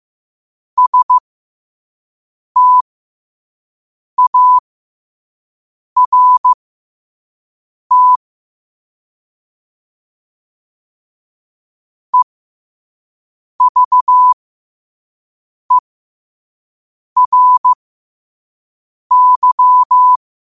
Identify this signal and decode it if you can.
Morse code, heard as a keyed tone, about 15 wpm, spaced as for 5 wpm, 1000 Hz. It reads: START EVERY
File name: qotdfiles / 5wpm